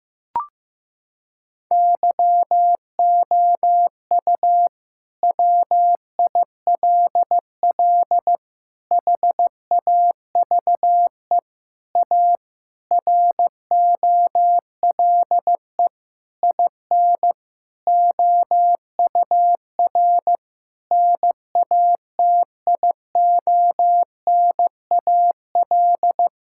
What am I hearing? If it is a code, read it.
Morse code, heard as a keyed tone, about 15 wpm, 700 Hz. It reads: YOU WILL HAVE A ROLE IN OUR NATIONAL
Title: Sentences from Top 1000 Words - Review Part 6 - 15wpm